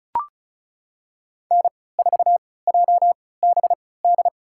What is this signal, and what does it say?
Morse code, heard as a keyed tone, about 36 wpm, spaced as for 20 wpm, 700 Hz. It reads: N4JBD
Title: USA Call Signs - 5 Characters - 3x Character Spacing - 35wpm